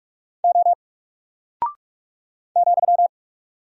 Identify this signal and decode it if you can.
Morse code, heard as a keyed tone, about 45 wpm, 700 Hz. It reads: O ,
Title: Single Character (Letter:Number:Punctuation:Prosign) Alt 45wpm